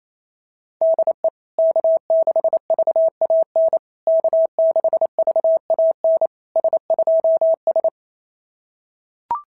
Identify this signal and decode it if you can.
Morse code, heard as a keyed tone, about 28 wpm, 660 Hz. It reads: DE K6VAN K6VAN S2S
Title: SOTA QSO Practice 28wpm